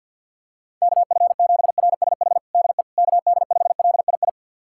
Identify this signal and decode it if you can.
Morse code, heard as a keyed tone, about 52 wpm, 700 Hz. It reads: KF7RSH DE KD5LII
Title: USA Callsign DE Callsign 50wpm